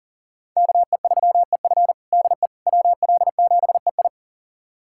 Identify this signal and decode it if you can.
Morse code, heard as a keyed tone, about 37 wpm, 700 Hz. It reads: KE3EF DE WL7EI
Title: USA Callsign DE Callsign - 5 Characters - 40wpm